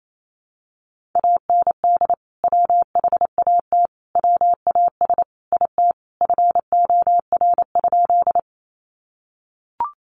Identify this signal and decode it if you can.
Morse code, heard as a keyed tone, about 28 wpm, 700 Hz. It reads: AND WHAT WAS IT FOR?
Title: Sentences from Top 100 Words 28wpm